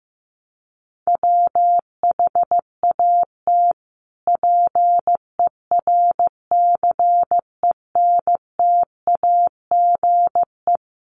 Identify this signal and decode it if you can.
Morse code, heard as a keyed tone, about 15 wpm, 700 Hz. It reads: WHAT PERCENTAGE
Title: Sentences from Top 1300 Words 15wpm